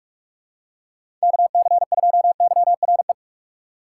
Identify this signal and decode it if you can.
Morse code, heard as a keyed tone, about 43 wpm, 700 Hz. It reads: KC2YRE
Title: USA Call Signs - 6 Characters - 2x Character Spacing - 45wpm